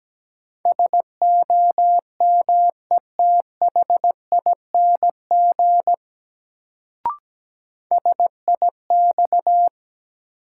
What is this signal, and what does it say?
Morse code, heard as a keyed tone, about 17 wpm, 700 Hz. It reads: SOMETHING SIX
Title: Sentences from Top 600 Words - New Word Drill - 17wpm